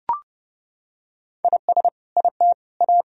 Morse code, heard as a keyed tone, about 30 wpm, 700 Hz. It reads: IS IT A
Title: Sentences from Top 1000 Words - Review Part 4 - 30wpm